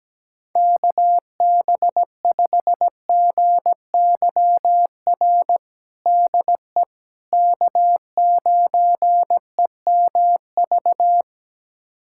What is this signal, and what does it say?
Morse code, heard as a keyed tone, about 17 wpm, 700 Hz. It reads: KB5GYR DE K9EMV
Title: USA Callsign DE Callsign 17wpm